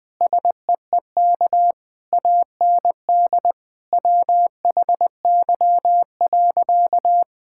Morse code, heard as a keyed tone, about 20 wpm, 700 Hz. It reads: SEEK AND WHY.
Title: Sentences from Top 1600 Words 20wpm